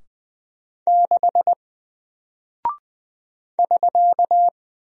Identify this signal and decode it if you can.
Morse code, heard as a keyed tone, about 20 wpm, 700 Hz. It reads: 6 <SK>
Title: Single Character (Letter:Number:Punctuation:Prosign) Alt 20wpm